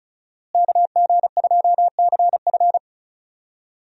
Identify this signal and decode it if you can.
Morse code, heard as a keyed tone, about 36 wpm, 700 Hz. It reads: KG2CF
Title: USA Call Signs - 5 Characters - Repeated 3x - 35wpm